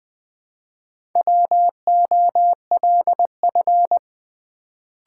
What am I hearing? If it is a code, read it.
Morse code, heard as a keyed tone, about 20 wpm, 700 Hz. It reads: WOLF
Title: Build A Word - 1 Letter At A Time - 4-Letter Words - 20wpm